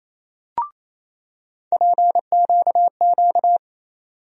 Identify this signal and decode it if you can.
Morse code, heard as a keyed tone, about 28 wpm, 700 Hz. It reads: PQQ
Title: USA Call Sign Postfixes - 3 Letters 28wpm